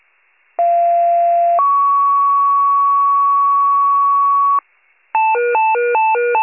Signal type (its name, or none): two-tone page